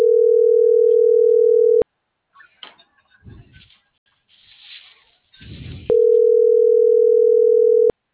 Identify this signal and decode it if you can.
Ringback tone (North America), 440 + 480 Hz.